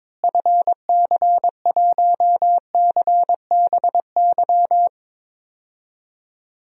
Morse code, heard as a keyed tone, about 22 wpm, 700 Hz. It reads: FC1CBY
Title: DXCC-100 Call Signs - 6 Characters - Repeated 2x - 22wpm